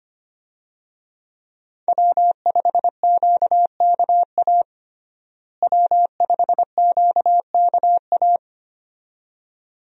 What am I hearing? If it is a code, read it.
Morse code, heard as a keyed tone, about 25 wpm, 700 Hz. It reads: W5QKA W5QKA
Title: USA Call Signs - 5 Characters - Repeated 2x - 25wpm